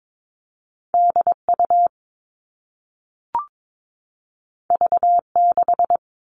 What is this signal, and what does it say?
Morse code, heard as a keyed tone, about 22 wpm, 700 Hz. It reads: DU V6
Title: All Cut Numbers - 2-Digit 22wpm